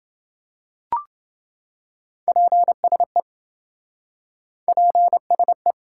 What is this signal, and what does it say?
Morse code, heard as a keyed tone, about 26 wpm, 700 Hz. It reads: PSE PSE